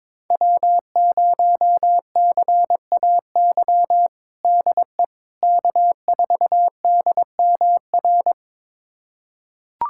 Morse code, heard as a keyed tone, about 21 wpm, 700 Hz. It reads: W0CAY DE K4DMR
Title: USA Callsign DE Callsign - 5 Characters - 22wpm